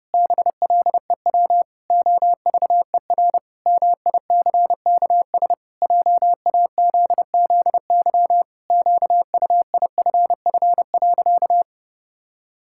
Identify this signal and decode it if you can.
Morse code, heard as a keyed tone, about 30 wpm, 700 Hz. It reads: BLEW OVER MICKS JAZZY QUIFF.
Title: Pangram Challenge v2 - 30wpm